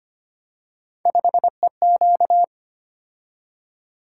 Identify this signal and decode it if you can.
Morse code, heard as a keyed tone, about 25 wpm, 700 Hz. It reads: 5EQ
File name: Three Letter-Number Combinations - 25wpm